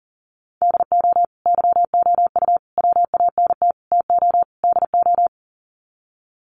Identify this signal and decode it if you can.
Morse code, heard as a keyed tone, about 40 wpm, 700 Hz. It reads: DO YOU WANT TO DO
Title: Sets of 5 Words Encore 40wpm